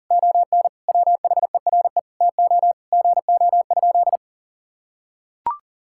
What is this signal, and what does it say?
Morse code, heard as a keyed tone, about 40 wpm, 700 Hz. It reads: ON WHERE TO GO?
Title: Sentences from Top 1600 Words 40wpm